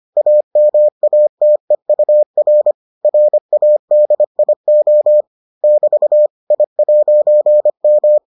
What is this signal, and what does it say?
Morse code, heard as a keyed tone, about 25 wpm, 600 Hz. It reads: AMATEUR RADIO <BT> I'M